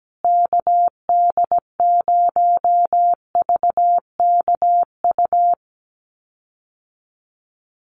Morse code, heard as a keyed tone, about 17 wpm, 700 Hz. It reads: KD0VKU
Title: USA Call Signs Repeated 2x 17wpm